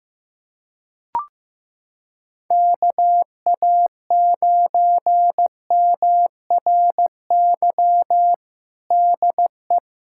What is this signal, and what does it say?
Morse code, heard as a keyed tone, about 16 wpm, 700 Hz. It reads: KA9MRY DE
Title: USA Callsign DE Callsign - 6 Characters - 15wpm